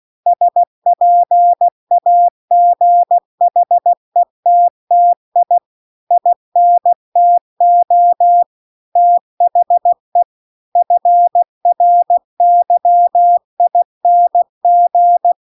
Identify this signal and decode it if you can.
Morse code, heard as a keyed tone, about 16 wpm, 700 Hz. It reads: SPAGHETTI INTO THE FRYING